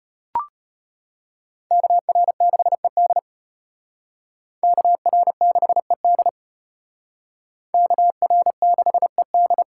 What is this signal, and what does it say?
Morse code, heard as a keyed tone, about 32 wpm, 700 Hz. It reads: KR6ED KR6ED KR6ED